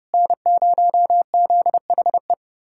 Morse code, heard as a keyed tone, about 33 wpm, 700 Hz. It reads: N0ZHE